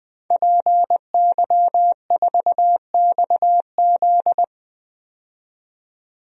Morse code, heard as a keyed tone, about 20 wpm, 700 Hz. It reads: PY4XZ
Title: DXCC-100 Call Signs - 5 Characters - 20wpm